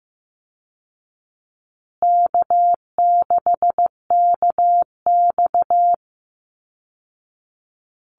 Morse code, heard as a keyed tone, about 15 wpm, 700 Hz. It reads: K6KX
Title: Call Signs Repeated 2x - 15wpm